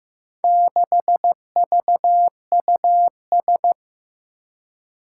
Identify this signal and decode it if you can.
Morse code, heard as a keyed tone, about 15 wpm, 700 Hz. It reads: 6VUS
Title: Four Challenging Letter-Number Combinations 15wpm